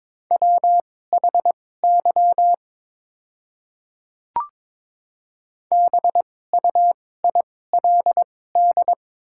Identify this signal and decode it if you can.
Morse code, heard as a keyed tone, about 22 wpm, spaced as for 16 wpm, 700 Hz. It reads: WHY BUILD
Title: 2x Character Spacing - Top 200 Words - 22wpm